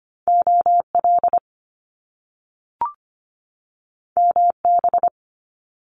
Morse code, heard as a keyed tone, about 25 wpm, 700 Hz. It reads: OL MB
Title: Top Two Letter Combinations in Words 25wpm